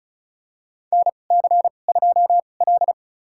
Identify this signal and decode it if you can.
Morse code, heard as a keyed tone, about 36 wpm, spaced as for 25 wpm, 700 Hz. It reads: NC2L
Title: USA Call Signs - 2x Character Spacing - 35wpm